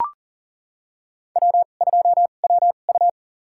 Morse code, heard as a keyed tone, about 40 wpm, spaced as for 29 wpm, 700 Hz. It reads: W2WU
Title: USA Call Signs - 4 Characters - 2x Character Spacing - 40wpm